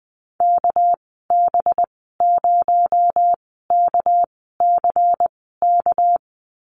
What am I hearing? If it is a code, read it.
Morse code, heard as a keyed tone, about 20 wpm, spaced as for 14 wpm, 700 Hz. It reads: KB0KCK
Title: 238 Lesson 38g - Letter q - Call Signs 20wpm